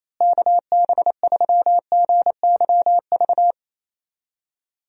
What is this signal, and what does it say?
Morse code, heard as a keyed tone, about 28 wpm, 700 Hz. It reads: KB3GYV
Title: USA Call Signs - 6 Characters - Repeated 2x - 28wpm